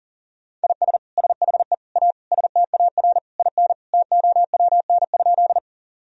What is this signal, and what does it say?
Morse code, heard as a keyed tone, about 40 wpm, 700 Hz. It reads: IS SHE A STAR IN TOWN?
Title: Sentences from Top 1000 Words - Review Part 2 - 40wpm